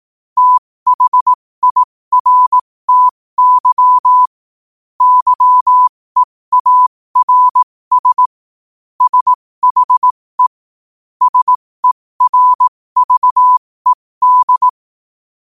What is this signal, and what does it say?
Morse code, heard as a keyed tone, about 18 wpm, spaced as for 15 wpm, 1000 Hz. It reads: THIRTY YEARS SHE SERVED
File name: qotdfiles / 15wpm